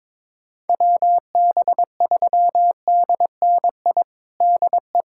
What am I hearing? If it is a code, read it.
Morse code, heard as a keyed tone, about 22 wpm, 700 Hz. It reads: WB3DNI DE
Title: USA Callsign DE Callsign - 6 Characters - 22wpm